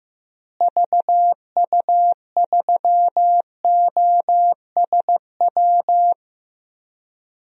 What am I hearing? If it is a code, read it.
Morse code, heard as a keyed tone, about 16 wpm, 700 Hz. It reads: VU3OSW
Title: DXCC-100 Call Signs - 6 Characters - Repeated 2x - 15wpm